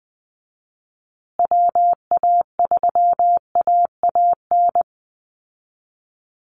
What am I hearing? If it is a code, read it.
Morse code, heard as a keyed tone, about 19 wpm, 700 Hz. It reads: WA3AAN
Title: Call Signs Repeated 2x - 20wpm